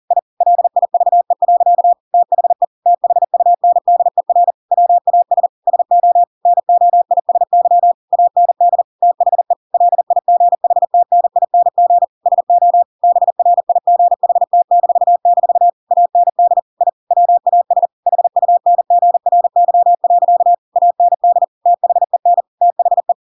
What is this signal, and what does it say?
Morse code, heard as a keyed tone, about 40 wpm, 700 Hz. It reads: I LIVE. THE THUNDER WAS SO NOISY AND THE LIGHTNING SO BRIGHT-- AND I WAS HUNGRY. AND THEN THE
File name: Pinocchio / Pinocchio ch7 40 40 ik1hge